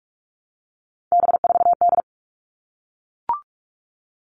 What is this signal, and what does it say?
Morse code, heard as a keyed tone, about 44 wpm, 700 Hz. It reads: B4D
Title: All Cut Numbers - 3-Digit 45wpm